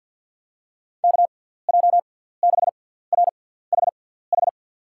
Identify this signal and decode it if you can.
Morse code, heard as a keyed tone, about 52 wpm, spaced as for 17 wpm, 700 Hz. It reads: KJ6RHH
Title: USA Call Signs - 6 Characters - 6x Character Spacing - 50wpm